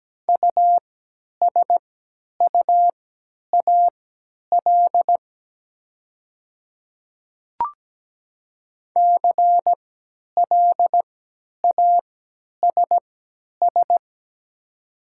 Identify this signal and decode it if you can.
Morse code, heard as a keyed tone, about 17 wpm, spaced as for 10 wpm, 700 Hz. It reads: USUAL CLASS
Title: 3x Character Spacing - Top 500 Words - 17wpm